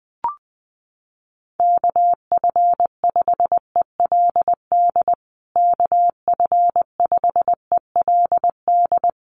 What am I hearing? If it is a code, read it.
Morse code, heard as a keyed tone, about 20 wpm, 700 Hz. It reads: KF5ELD KF5ELD
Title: USA Call Signs Repeated 2x 20wpm